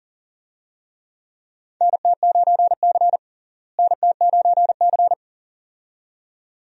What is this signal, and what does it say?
Morse code, heard as a keyed tone, about 40 wpm, 700 Hz. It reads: NT9C NT9C
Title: USA Call Signs - 4 Characters - Repeated 2x - 40wpm